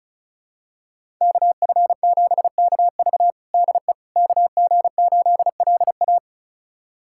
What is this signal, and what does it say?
Morse code, heard as a keyed tone, about 36 wpm, 700 Hz. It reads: KF7KV DE KG8LA
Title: USA Callsign DE Callsign - 5 Characters - 35wpm